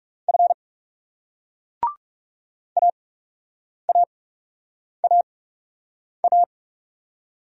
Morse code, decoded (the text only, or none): F AAAA